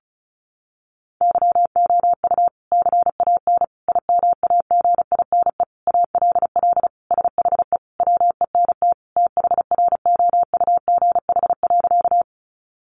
Morse code, heard as a keyed tone, about 34 wpm, 700 Hz. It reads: YOU CAN IMAGINE ALL SHE WENT THROUGH.
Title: Sentences from Top 1400 Words 35wpm